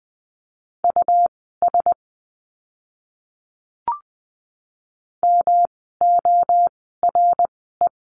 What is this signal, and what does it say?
Morse code, heard as a keyed tone, about 20 wpm, spaced as for 14 wpm, 700 Hz. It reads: US MORE
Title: 160 Lesson 26f - Letter g - Words 20wpm